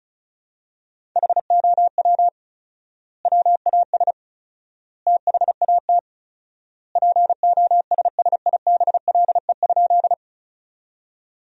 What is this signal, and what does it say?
Morse code, heard as a keyed tone, about 36 wpm, 700 Hz. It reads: HOW WAS THAT POSSIBLE?
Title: 4x Word-Spacing - Sentences from Top 500 Words - 35wpm